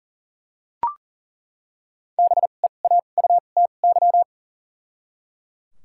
Morse code, heard as a keyed tone, about 37 wpm, 700 Hz. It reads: B E A U T Y
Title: 2x Character Spacing - Top 500 Words - 40wpm